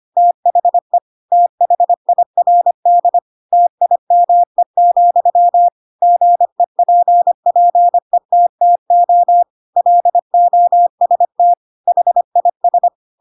Morse code, heard as a keyed tone, about 25 wpm, 700 Hz. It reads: THE THIRD TIME, GEPPETTO LOST HIS
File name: Pinocchio / Pinocchio ch2 25 25 ik1hge